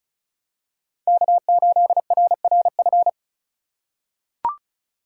Morse code, heard as a keyed tone, about 36 wpm, 700 Hz. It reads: K8RRF
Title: USA Call Signs - 3x Character Spacing - 35wpm